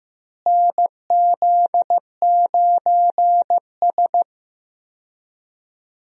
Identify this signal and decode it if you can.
Morse code, heard as a keyed tone, about 16 wpm, 700 Hz. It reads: NZ9S
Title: USA Call Signs - 4 Characters - Repeated 3x - 15wpm